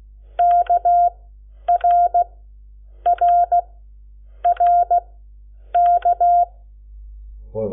Morse code, heard as a keyed tone, about 16 wpm, spaced as for 8 wpm, 700 Hz. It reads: KRRRK
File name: Lecciones / CLASE Nº7